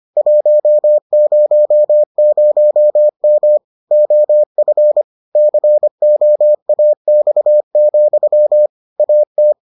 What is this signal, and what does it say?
Morse code, heard as a keyed tone, about 25 wpm, 600 Hz. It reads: 100M OF COAX, AT